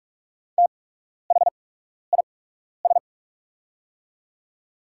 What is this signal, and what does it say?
Morse code, heard as a keyed tone, about 44 wpm, spaced as for 12 wpm, 690 Hz. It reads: THIS
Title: 8x Character Spacing - Top 500 Words - 45wpm